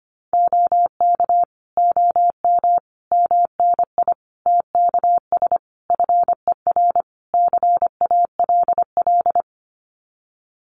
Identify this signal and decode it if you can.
Morse code, heard as a keyed tone, about 25 wpm, 700 Hz. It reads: OK OM MNI TKS FER CALL
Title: Common QSO Phrases 25wpm